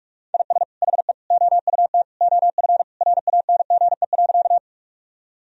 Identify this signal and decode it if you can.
Morse code, heard as a keyed tone, about 41 wpm, 700 Hz. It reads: IS HE OUT OF RANGE.
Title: Sentences from Top 1000 Words - Review Part 4 - 45wpm